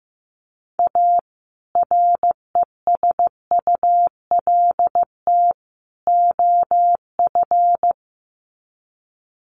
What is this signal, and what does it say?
Morse code, heard as a keyed tone, about 15 wpm, 700 Hz. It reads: A RESULT OF
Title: Sets of 3 Words 15wpm